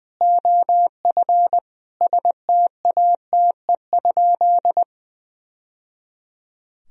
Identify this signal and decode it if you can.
Morse code, heard as a keyed tone, about 20 wpm, 700 Hz. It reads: OF STATE?
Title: Sentences from Top 1000 Words - Review Part 2 - 20wpm